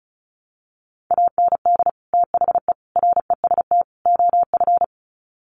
Morse code, heard as a keyed tone, about 34 wpm, 700 Hz. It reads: AND THE REST OF
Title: Sets of 4 Words 35wpm